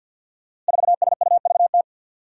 Morse code, heard as a keyed tone, about 46 wpm, 700 Hz. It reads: 4SUVT
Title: Five Challenging Letter-Number Combinations 50wpm